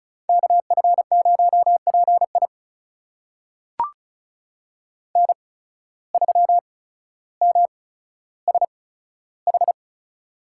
Morse code, heard as a keyed tone, about 34 wpm, spaced as for 15 wpm, 700 Hz. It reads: KF0PI N3MSH
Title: USA Call Signs - 5 Characters - 8x Character Spacing - 35wpm